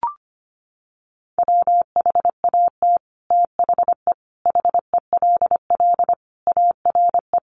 Morse code, heard as a keyed tone, about 25 wpm, 700 Hz. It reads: WHAT THE HELL ARE
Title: Sets of 5 Words Encore 25wpm